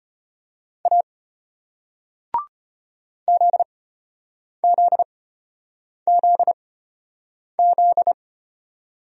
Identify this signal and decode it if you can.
Morse code, heard as a keyed tone, about 33 wpm, spaced as for 8 wpm, 700 Hz. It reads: A ZZZZ